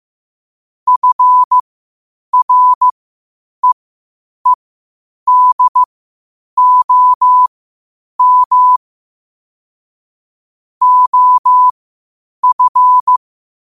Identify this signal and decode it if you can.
Morse code, heard as a keyed tone, about 15 wpm, spaced as for 8 wpm, 1000 Hz. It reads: FREEDOM OF
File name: qotdfiles / 8wpm